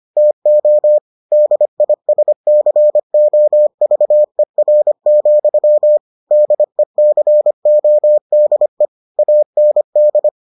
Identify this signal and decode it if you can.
Morse code, heard as a keyed tone, about 25 wpm, 600 Hz. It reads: TO DISCOVER, DECODE AND